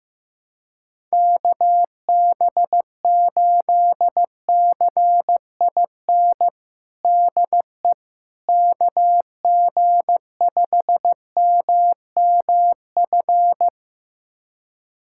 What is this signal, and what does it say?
Morse code, heard as a keyed tone, about 15 wpm, 700 Hz. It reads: KB8CIN DE KG5MMF
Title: USA Callsign DE Callsign - 6 Characters - 15wpm